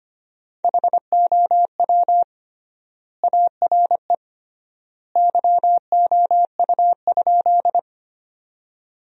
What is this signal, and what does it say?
Morse code, heard as a keyed tone, about 25 wpm, 700 Hz. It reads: HOW ARE YOU?
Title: 3x Word-Spacing - Sentences from Top 100 Words - 25wpm